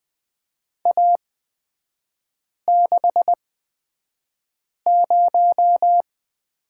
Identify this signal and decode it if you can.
Morse code, heard as a keyed tone, about 20 wpm, spaced as for 5 wpm, 700 Hz. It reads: A60
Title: 224 Lesson 36e - BT - Three Characters 20wpm